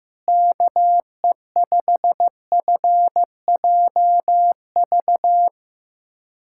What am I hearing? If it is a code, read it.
Morse code, heard as a keyed tone, about 16 wpm, 700 Hz. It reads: KE5FJV
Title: USA Callsign DE Callsign - 6 Characters - 15wpm